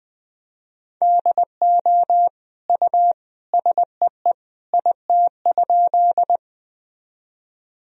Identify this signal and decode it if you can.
Morse code, heard as a keyed tone, about 20 wpm, 700 Hz. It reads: DO U SEE IT?
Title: Sentences from Top 100 Words and Abbreviations 20wpm